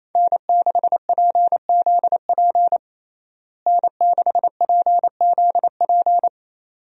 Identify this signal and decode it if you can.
Morse code, heard as a keyed tone, about 28 wpm, 700 Hz. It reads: N6PZP N6PZP
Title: USA Call Signs - 5 Characters - Repeated 2x - 28wpm